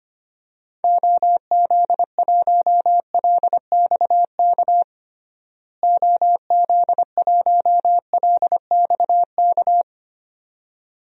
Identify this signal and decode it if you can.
Morse code, heard as a keyed tone, about 25 wpm, 700 Hz. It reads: OZ1LXK OZ1LXK
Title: DXCC-100 Call Signs - 6 Characters - Repeated 2x - 25wpm